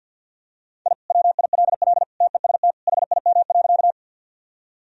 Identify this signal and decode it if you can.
Morse code, heard as a keyed tone, about 52 wpm, 700 Hz. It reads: I WILL TEST HIM.
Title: Sentences from Top 1000 Words - Review Part 3 - 50wpm